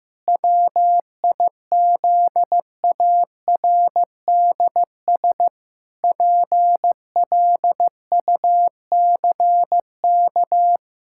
Morse code, heard as a keyed tone, about 16 wpm, 700 Hz. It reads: WIZARDS PLUCK
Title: Pangram Challenge v2 - 15wpm